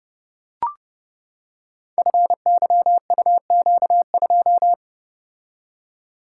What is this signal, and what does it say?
Morse code, heard as a keyed tone, about 31 wpm, 700 Hz. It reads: FYUQ2
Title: Code Groups - 5 Letter:Numbers - 30wpm